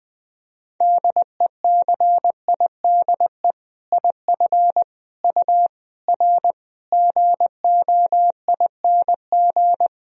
Morse code, heard as a keyed tone, about 20 wpm, 700 Hz. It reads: DECIDE IF U R GOING
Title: Sentences from Top 500 Words and Abbreviations 20wpm